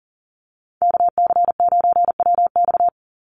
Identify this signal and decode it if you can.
Morse code, heard as a keyed tone, about 40 wpm, 700 Hz. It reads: KC9WX
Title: USA Call Signs 40wpm